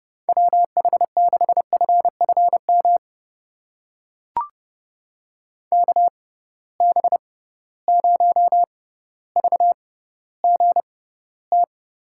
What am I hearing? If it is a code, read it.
Morse code, heard as a keyed tone, about 30 wpm, spaced as for 15 wpm, 700 Hz. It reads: WH6FFM KB0VGT
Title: USA Call Signs - 6 Characters - 6x Character Spacing - 30wpm